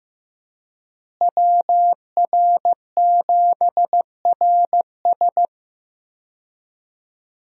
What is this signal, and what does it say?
Morse code, heard as a keyed tone, about 15 wpm, 700 Hz. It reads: WR7RS
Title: USA Call Signs - 5 Characters - Repeated 3x - 15wpm